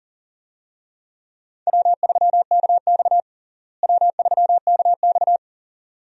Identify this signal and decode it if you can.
Morse code, heard as a keyed tone, about 40 wpm, 700 Hz. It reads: W3KX W3KX
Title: USA Call Signs - 4 Characters - Repeated 2x - 40wpm